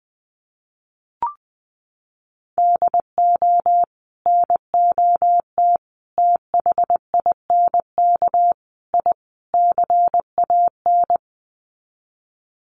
Morse code, heard as a keyed tone, about 20 wpm, 700 Hz. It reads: DO NOT THINK I CAN
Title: Sets of 5 Words Encore2 20wpm